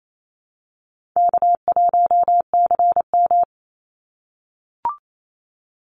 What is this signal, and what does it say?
Morse code, heard as a keyed tone, about 28 wpm, 700 Hz. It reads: K1CM
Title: NA SOTA Chaser Callsigns 28wpm